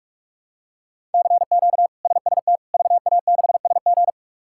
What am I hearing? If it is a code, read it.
Morse code, heard as a keyed tone, about 46 wpm, 700 Hz. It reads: CQ SST VA6SG
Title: K1USN SST QSO Practice - 45wpm